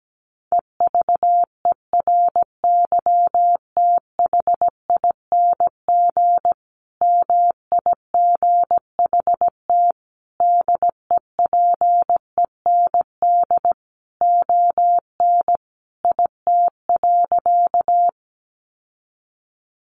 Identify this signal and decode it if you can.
Morse code, heard as a keyed tone, about 17 wpm, 700 Hz. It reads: EVERYTHING MIGHT DEPEND ON IT.
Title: Sentences from Top 1300 Words 17wpm